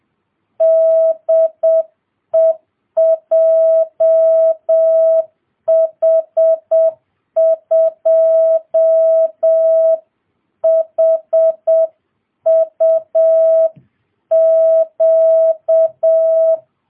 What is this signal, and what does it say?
Morse code, heard as a keyed tone, about 7 wpm, 650 Hz. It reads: DE JH2HUQ